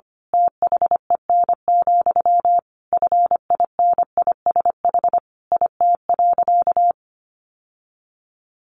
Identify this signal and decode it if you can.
Morse code, heard as a keyed tone, about 25 wpm, 700 Hz. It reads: THEN, FINISH IT.
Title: Sentences from Top 1200 Words 25wpm